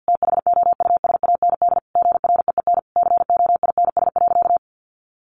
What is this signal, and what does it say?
Morse code, heard as a keyed tone, about 50 wpm, 700 Hz. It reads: THOUSAND GREEN COINS.
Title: A Sound Of Thunder 50wpm